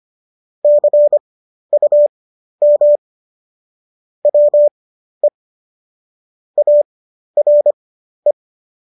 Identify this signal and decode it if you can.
Morse code, heard as a keyed tone, about 25 wpm, spaced as for 12 wpm, 590 Hz. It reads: CUM WE ARE